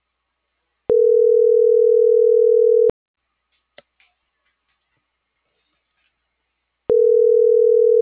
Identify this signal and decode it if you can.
Ringback tone (North America), 440 + 480 Hz.